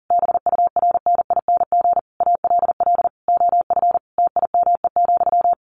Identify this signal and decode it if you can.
Morse code, heard as a keyed tone, about 40 wpm, 700 Hz. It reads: BURNING ALL OF TIME,
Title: A Sound Of Thunder 40wpm